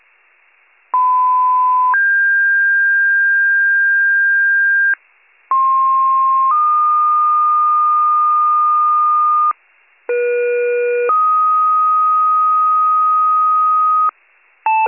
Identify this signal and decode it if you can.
Two-tone page, heard as a one-second tone, then a three-second tone, about 1010 then 1600 Hz, about 1060 then 1220 Hz, about 480 then 1250 Hz. It is heard from 0.9 to 14.1 s.